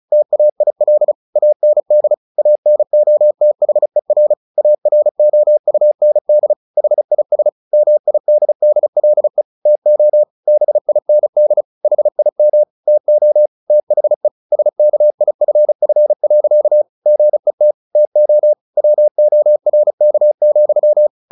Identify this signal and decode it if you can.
Morse code, heard as a keyed tone, about 35 wpm, 600 Hz. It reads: TAIL AND ANOTHER AROUND HIS MIDDLE TO BIND HIM TO THE SKIFF. GET TO WORK,
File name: the old man and the sea 2 35wpm